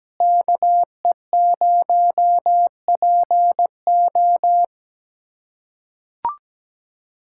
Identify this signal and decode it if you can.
Morse code, heard as a keyed tone, about 17 wpm, 700 Hz. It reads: KE0PO
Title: USA Callsign DE Callsign - 5 Characters - 17wpm